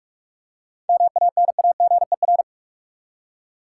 Morse code, heard as a keyed tone, about 46 wpm, 700 Hz. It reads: MANAGER
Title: Seven Letter Words - 45wpm